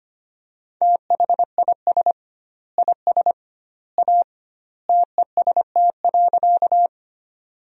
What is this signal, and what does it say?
Morse code, heard as a keyed tone, about 25 wpm, 700 Hz. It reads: THIS IS A TEST.
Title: 2x Word-Spacing - Sentences from Top 500 Words - 25wpm